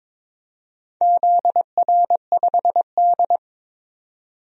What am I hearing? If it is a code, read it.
Morse code, heard as a keyed tone, about 22 wpm, 700 Hz. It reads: ZR5D
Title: DXCC-100 Call Signs - 4 Characters - 22wpm